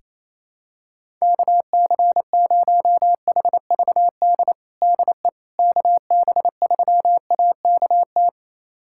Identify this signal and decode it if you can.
Morse code, heard as a keyed tone, about 28 wpm, 700 Hz. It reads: KC0HVD DE KB3AKT
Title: USA Callsign DE Callsign 28wpm